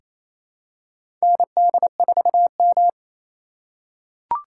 Morse code, heard as a keyed tone, about 27 wpm, 700 Hz. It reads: ND4M
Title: USA Call Signs - 2x Character Spacing - 28wpm